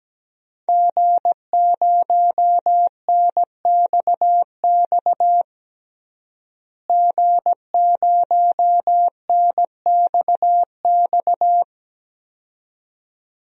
Morse code, heard as a keyed tone, about 17 wpm, 700 Hz. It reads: G0NXX G0NXX
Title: DXCC-100 Call Signs - 5 Characters - Repeated 2x - 17wpm